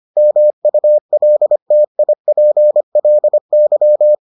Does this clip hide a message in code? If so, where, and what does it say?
Yes: Morse code from 0.2 to 4.2 s, heard as a keyed tone, about 25 wpm, 600 Hz. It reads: MULTIPLY